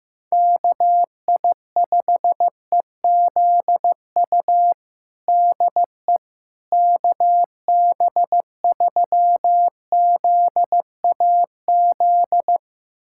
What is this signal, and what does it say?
Morse code, heard as a keyed tone, about 15 wpm, 700 Hz. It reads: KI5EZU DE KB3ZAZ
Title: USA Callsign DE Callsign - 6 Characters - 15wpm